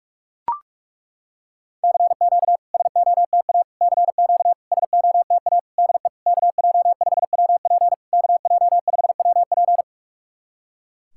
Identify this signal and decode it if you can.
Morse code, heard as a keyed tone, about 42 wpm, 700 Hz. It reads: CQ SOTA CQ SOTA DE KJ5WP KJ5WP
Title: SOTA QSO Practice 45wpm